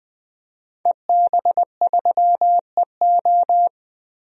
Morse code, heard as a keyed tone, about 20 wpm, 700 Hz. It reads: EB3EO
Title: DXCC-100 Call Signs - 5 Characters - 20wpm